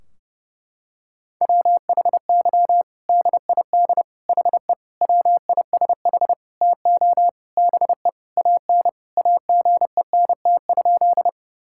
Morse code, heard as a keyed tone, about 30 wpm, 700 Hz. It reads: WHY DID HE WISH TO BE AN AGENT?
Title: Sentences from Top 1600 Words 30wpm